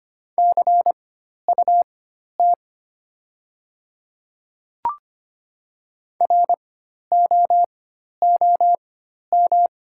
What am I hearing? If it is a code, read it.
Morse code, heard as a keyed tone, about 25 wpm, spaced as for 12 wpm, 700 Hz. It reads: CUT ROOM
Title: 4x Character Spacing - Top 300 Words - 25wpm